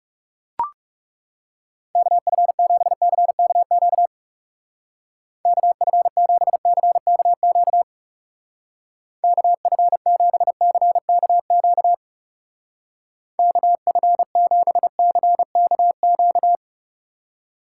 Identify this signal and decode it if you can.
Morse code, heard as a keyed tone, about 37 wpm, 700 Hz. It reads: KF7CKQ KF7CKQ KF7CKQ KF7CKQ
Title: Speed Racing - USA 2x3 Call Signs 30wpm